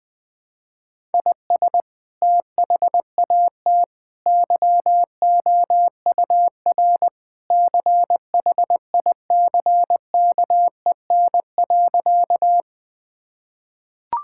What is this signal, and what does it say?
Morse code, heard as a keyed tone, about 20 wpm, 700 Hz. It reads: IS THAT YOUR CHICKEN.
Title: Sentences from Top 1000 Words - Review Part 5 - 20wpm